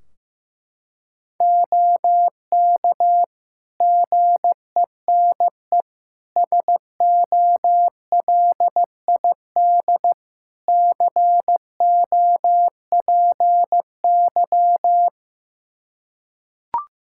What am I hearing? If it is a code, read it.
Morse code, heard as a keyed tone, about 15 wpm, 700 Hz. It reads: OK GENE SOLID COPY
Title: Practice QSO - Exchange 3 - More Info - 15wpm